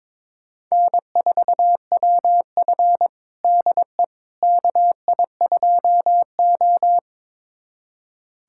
Morse code, heard as a keyed tone, about 21 wpm, 700 Hz. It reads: N4WF DE KI2O
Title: USA Callsign DE Callsign - 4 Characters - 22wpm